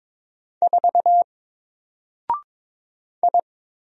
Morse code, heard as a keyed tone, about 22 wpm, 700 Hz. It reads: 4 I
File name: Single-Letter-Number-22wpm